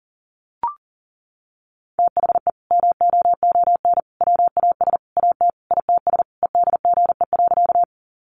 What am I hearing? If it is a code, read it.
Morse code, heard as a keyed tone, about 40 wpm, 700 Hz. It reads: THE MOON WAS AT ITS EDGE.
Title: Sentences from Top 1300 Words 40wpm